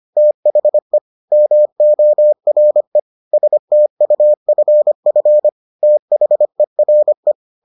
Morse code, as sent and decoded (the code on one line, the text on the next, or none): - .... / -- --- .-. / ... - ..- ..-. ..-. / - .... .-. .
THE MORE STUFF THERE